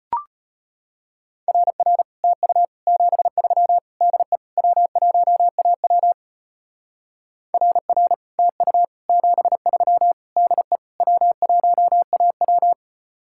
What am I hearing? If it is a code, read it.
Morse code, heard as a keyed tone, about 32 wpm, 700 Hz. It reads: RR TU 73 DE W1AW RR TU 73 DE W1AW